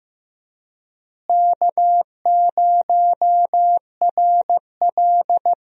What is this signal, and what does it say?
Morse code, heard as a keyed tone, about 15 wpm, 700 Hz. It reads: K0RL
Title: USA Call Signs - 4 Characters - Repeated 2x - 15wpm